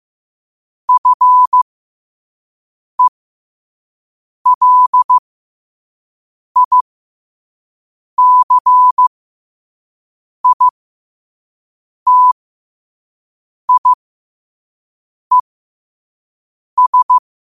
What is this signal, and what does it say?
Morse code, heard as a keyed tone, about 15 wpm, spaced as for 5 wpm, 1000 Hz. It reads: FELICITIES
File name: qotdfiles / 5wpm